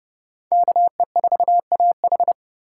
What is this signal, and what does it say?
Morse code, heard as a keyed tone, about 27 wpm, 700 Hz. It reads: KE4AH